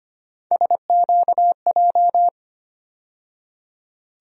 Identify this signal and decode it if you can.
Morse code, heard as a keyed tone, about 25 wpm, 700 Hz. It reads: SQJ
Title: USA Call Sign Postfixes - 3 Letters 25wpm